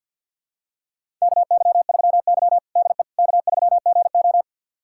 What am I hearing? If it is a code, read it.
Morse code, heard as a keyed tone, about 52 wpm, 700 Hz. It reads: KY3Y DE K3GO
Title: USA Callsign DE Callsign - 4 Characters - 50wpm